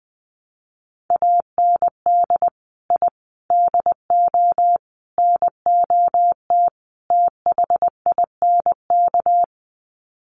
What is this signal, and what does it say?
Morse code, heard as a keyed tone, about 20 wpm, 700 Hz. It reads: AND I DO NOT THINK
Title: Sets of 5 Words 20wpm